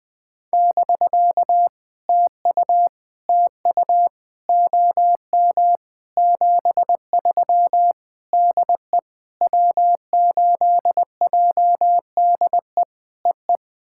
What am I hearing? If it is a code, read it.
Morse code, heard as a keyed tone, about 20 wpm, 700 Hz. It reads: <BK> TU TU OM 73 DE W8JDE EE
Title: POTA QSO Practice - 20wpm